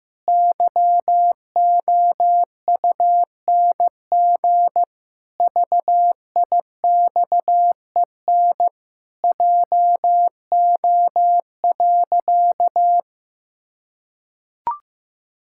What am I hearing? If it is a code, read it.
Morse code, heard as a keyed tone, about 15 wpm, 700 Hz. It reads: YOUNG VIXEN JO.
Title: Pangram Challenge v2 - 15wpm